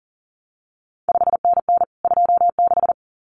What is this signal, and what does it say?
Morse code, heard as a keyed tone, about 42 wpm, 700 Hz. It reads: 5NN 26
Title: CQ WW DX Contest Exchange Encore - 40wpm